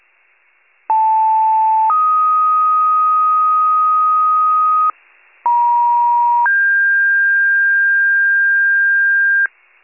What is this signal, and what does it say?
Two-tone page, heard as a one-second tone, then a three-second tone, about 880 then 1250 Hz, about 950 then 1600 Hz, with noise. It is heard from 0.9 to 9.5 s.
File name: wcalert